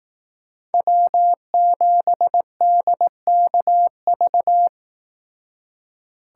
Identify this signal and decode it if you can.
Morse code, heard as a keyed tone, about 18 wpm, 700 Hz. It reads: W7DKV
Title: Speed Racing - USA 1x3 Call Signs 15wpm